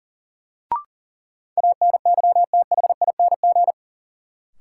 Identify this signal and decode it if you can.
Morse code, heard as a keyed tone, about 40 wpm, 700 Hz. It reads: ANYTHING
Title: Sentences from Top 1000 Words - New Word Drill - 40wpm